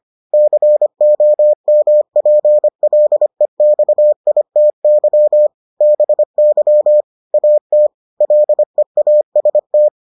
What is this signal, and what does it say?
Morse code, heard as a keyed tone, about 25 wpm, 600 Hz. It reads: COMPLEXITY BY AT LEAST